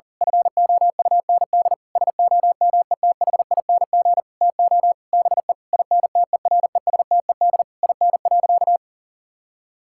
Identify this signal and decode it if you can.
Morse code, heard as a keyed tone, about 40 wpm, 700 Hz. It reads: FOUND SOMETHING TO BE INTERESTED IN.
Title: Sentences from Top 1000 Words - Review Part 6 - 40wpm